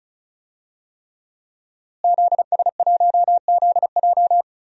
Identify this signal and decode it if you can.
Morse code, heard as a keyed tone, about 36 wpm, 700 Hz. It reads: ZS1ZJ
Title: DXCC-100 Call Signs - 5 Characters - Repeated 3x - 35wpm